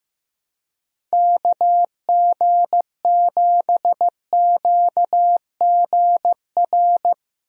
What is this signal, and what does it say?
Morse code, heard as a keyed tone, about 15 wpm, 700 Hz. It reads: KG7QGR
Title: USA Call Signs - 6 Characters - 15wpm